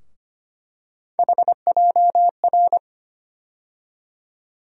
Morse code, heard as a keyed tone, about 25 wpm, 700 Hz. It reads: HJR
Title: USA Call Sign Postfixes - 3 Letters 25wpm